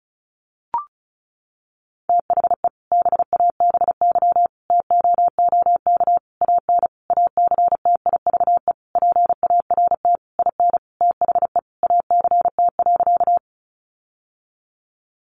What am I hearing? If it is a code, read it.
Morse code, heard as a keyed tone, about 34 wpm, 700 Hz. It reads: THE BABY TOOK AN ACTIVE PART IN THE ACT.
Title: Sentences from Top 1300 Words 35wpm